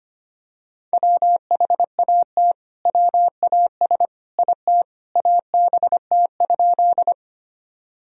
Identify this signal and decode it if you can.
Morse code, heard as a keyed tone, about 25 wpm, 700 Hz. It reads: WHAT WAS IT ABT?
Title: Sentences from Top 100 Words and Abbreviations 25wpm